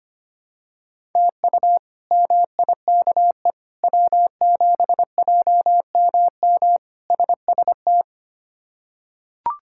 Morse code, heard as a keyed tone, about 25 wpm, 700 Hz. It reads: TU MIKE W7JMM SST
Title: K1USN SST QSO Practice - 25wpm